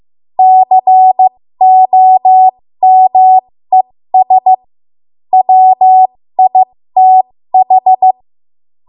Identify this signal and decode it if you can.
Morse code, heard as a keyed tone, about 15 wpm, spaced as for 13 wpm, 750 Hz. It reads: COMES WITH
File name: Morse / 13WPM